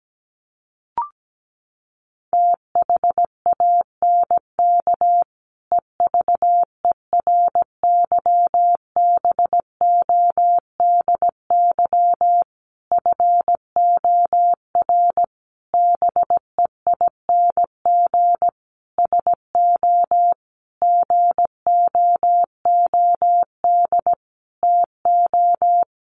Morse code, heard as a keyed tone, about 17 wpm, 700 Hz. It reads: THANK EVERYBODY FOR BEING SO GOOD TO
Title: Sentences from Top 1400 Words 17wpm